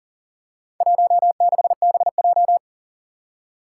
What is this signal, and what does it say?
Morse code, heard as a keyed tone, about 42 wpm, 700 Hz. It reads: 16BJ
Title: Four Challenging Letter-Number Combinations 40wpm